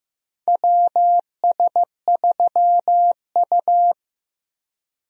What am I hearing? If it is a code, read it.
Morse code, heard as a keyed tone, about 16 wpm, 700 Hz. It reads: WS3U
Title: USA Call Signs - 4 Characters - 15wpm